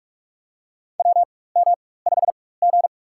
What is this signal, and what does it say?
Morse code, heard as a keyed tone, about 44 wpm, spaced as for 21 wpm, 700 Hz. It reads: WM5G
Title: USA Call Signs - 4 Characters - 4x Character Spacing - 45wpm